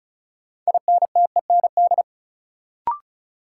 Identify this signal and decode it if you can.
Morse code, heard as a keyed tone, about 36 wpm, 700 Hz. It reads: INTEND
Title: Sentences from Top 1600 Words - New Word Drill - 35wpm